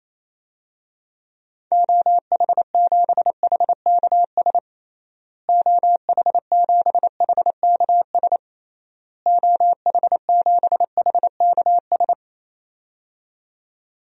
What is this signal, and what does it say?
Morse code, heard as a keyed tone, about 28 wpm, 700 Hz. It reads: OH7HKS OH7HKS OH7HKS
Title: DXCC-100 Call Signs - 6 Characters - Repeated 3x - 28wpm